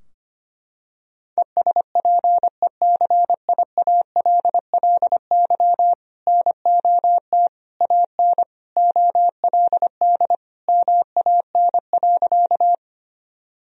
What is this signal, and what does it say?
Morse code, heard as a keyed tone, about 25 wpm, 700 Hz. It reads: ESPECIALLY NOT AN OLD MAN.
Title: Sentences from Top 1000 Words - Review Part 4 - 25wpm